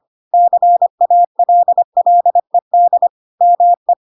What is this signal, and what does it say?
Morse code, heard as a keyed tone, about 25 wpm, 700 Hz. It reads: CALLED ME
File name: Pinocchio / Pinocchio ch2 25 25 ik1hge